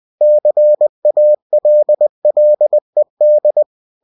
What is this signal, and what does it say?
Morse code, heard as a keyed tone, about 20 wpm, 600 Hz. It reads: CALLED
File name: cw-books / winnie-the-pooh / winnie-20wpm-page-0080